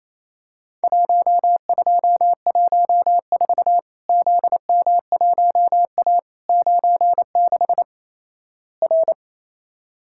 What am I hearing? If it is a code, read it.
Morse code, heard as a keyed tone, about 28 wpm, 700 Hz. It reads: 1214 ZM1A 96 R
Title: Worked All Europe DX Contest - QTC Exchange - 28wpm